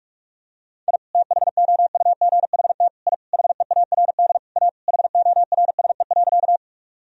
Morse code, heard as a keyed tone, about 46 wpm, 700 Hz. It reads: I THOUGHT I HEARD A HORSE.
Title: Sentences from Top 1000 Words - Review Part 1 - 45wpm